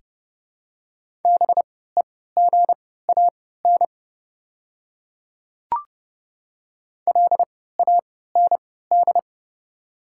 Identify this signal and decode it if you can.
Morse code, heard as a keyed tone, about 30 wpm, spaced as for 17 wpm, 690 Hz. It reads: BEGAN LAND
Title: 3x Character Spacing - Top 500 Words - 30wpm